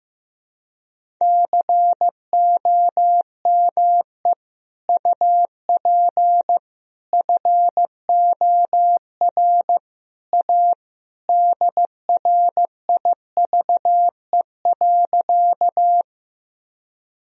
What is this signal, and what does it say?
Morse code, heard as a keyed tone, about 15 wpm, 700 Hz. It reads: COME UP FOR A DRIVE.
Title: Sentences from Top 1000 Words - Review Part 2 - 15wpm